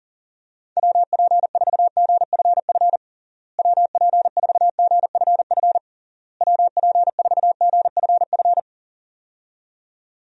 Morse code, heard as a keyed tone, about 40 wpm, 700 Hz. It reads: WP4GFF WP4GFF WP4GFF
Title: USA Call Signs - 6 Characters - Repeated 3x - 40wpm